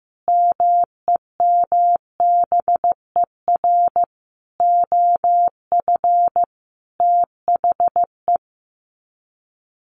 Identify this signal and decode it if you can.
Morse code, heard as a keyed tone, about 15 wpm, 700 Hz. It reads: MEMBER OF THE
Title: Sets of 5 Words 15wpm